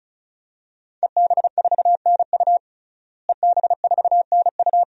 Morse code, heard as a keyed tone, about 34 wpm, 700 Hz. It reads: EB4NU EB4NU
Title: DXCC-100 Call Signs - 5 Characters - Repeated 2x - 35wpm